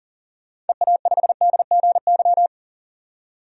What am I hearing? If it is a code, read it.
Morse code, heard as a keyed tone, about 40 wpm, 700 Hz. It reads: EA5DGY
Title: DXCC-100 Call Signs - 6 Characters - 40wpm